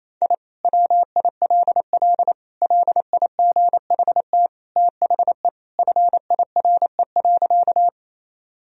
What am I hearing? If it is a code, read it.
Morse code, heard as a keyed tone, about 28 wpm, 700 Hz. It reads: I WILL LIGHT THE FIRE.
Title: Sentences from Top 1000 Words - Review Part 1 - 28wpm